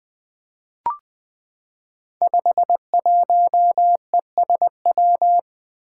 Morse code, heard as a keyed tone, about 20 wpm, 700 Hz. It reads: 51ESW
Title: Five Challenging Letter-Number Combinations 20wpm